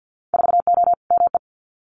Morse code, heard as a keyed tone, about 50 wpm, 700 Hz. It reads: 4O ME
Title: Winter Field Day Exchanges Encore 50wpm